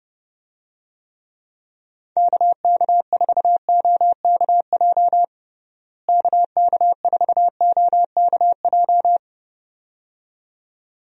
Morse code, heard as a keyed tone, about 31 wpm, 700 Hz. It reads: KK4OKJ KK4OKJ
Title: USA Call Signs - 6 Characters - Repeated 2x - 30wpm